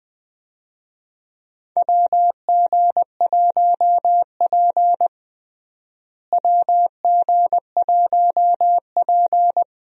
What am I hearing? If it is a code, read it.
Morse code, heard as a keyed tone, about 21 wpm, 700 Hz. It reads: WG1P WG1P
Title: USA Call Signs - 4 Characters - Repeated 2x - 20wpm